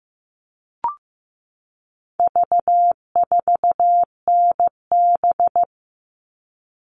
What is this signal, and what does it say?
Morse code, heard as a keyed tone, about 15 wpm, 700 Hz. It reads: V4NB
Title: All Cut Numbers - 4-Digit 15wpm